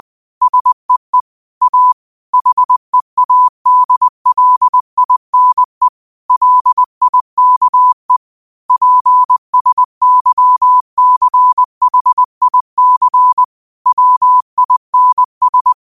Morse code, heard as a keyed tone, about 20 wpm, 1000 Hz. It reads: SEE A HEADLINE LIKE PSYCHIC WINS
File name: qotdfiles / 20wpm